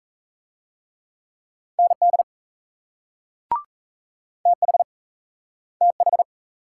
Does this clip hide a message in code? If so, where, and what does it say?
Yes: Morse code from 1.8 to 6.2 s, heard as a keyed tone, about 38 wpm, 700 Hz. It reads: ND TH TH